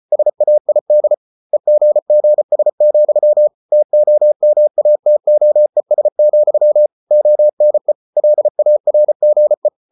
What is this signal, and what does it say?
Morse code, heard as a keyed tone, about 34 wpm, 600 Hz. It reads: SAID EGGS, TOMATOES, ONE LARGE